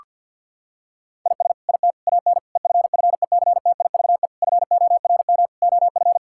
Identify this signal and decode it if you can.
Morse code, heard as a keyed tone, about 50 wpm, 700 Hz. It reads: IS IT AN EFFECTIVE FORM OF
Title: Sentences from Top 1000 Words - Review Part 6 - 50wpm